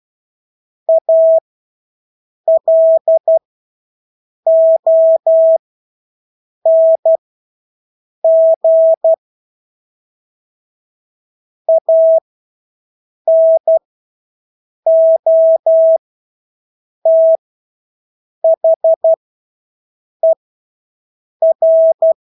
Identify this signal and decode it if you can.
Morse code, heard as a keyed tone, about 12 wpm, spaced as for 6 wpm, 650 Hz. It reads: ALONG ANOTHER
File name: Engelske ord 12-6 wpm